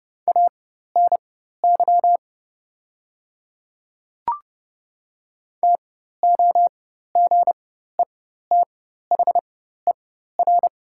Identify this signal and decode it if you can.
Morse code, heard as a keyed tone, about 30 wpm, spaced as for 14 wpm, 700 Hz. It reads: ANY TOGETHER
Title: 4x Character Spacing - Top 300 Words - 30wpm